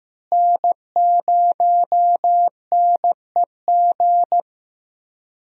Morse code, heard as a keyed tone, about 15 wpm, 700 Hz. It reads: N0NEG
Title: USA Call Signs - 5 Characters - Repeated 3x - 15wpm